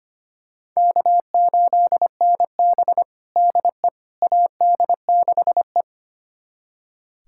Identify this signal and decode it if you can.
Morse code, heard as a keyed tone, about 25 wpm, 700 Hz. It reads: K8NB DE AD6E
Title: USA Callsign DE Callsign - 4 Characters - 25wpm